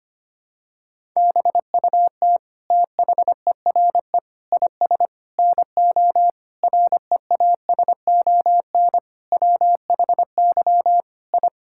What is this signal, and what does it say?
Morse code, heard as a keyed tone, about 25 wpm, 700 Hz. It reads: BUT THERE IS NO REASON WHY I
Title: Sentences from Top 1000 Words - Review Part 6 - 25wpm